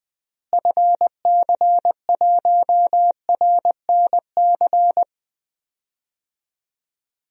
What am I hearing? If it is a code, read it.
Morse code, heard as a keyed tone, about 20 wpm, 700 Hz. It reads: FC1RNC
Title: DXCC-100 Call Signs - 6 Characters - Repeated 3x - 20wpm